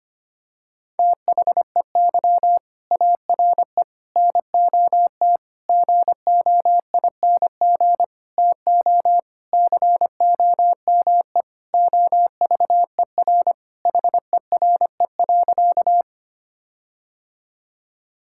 Morse code, heard as a keyed tone, about 25 wpm, 700 Hz. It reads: THEY ARE NOT GOING TO COME OVER HERE.
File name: practice / 25wpm / Sentences with 4-Word Phrases - 25wpm